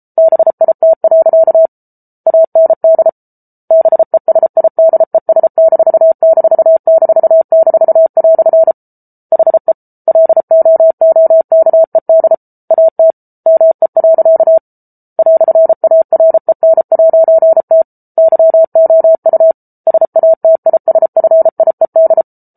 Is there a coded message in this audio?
Yes: Morse code from 0.2 to 22.2 s, heard as a keyed tone, about 34 wpm, 650 Hz. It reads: BIT. AND BESIDES----" HE LOOKED AT ME. "AREN'T YOU SATISFIED